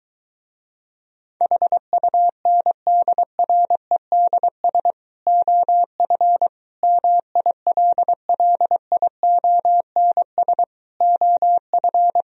Morse code, heard as a keyed tone, about 23 wpm, 700 Hz. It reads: HUNDREDS OF MILLIONS OF